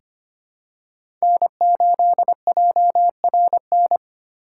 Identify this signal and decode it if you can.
Morse code, heard as a keyed tone, about 25 wpm, 700 Hz. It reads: N8JRN
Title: USA Call Signs - 5 Characters - 2x Character Spacing - 25wpm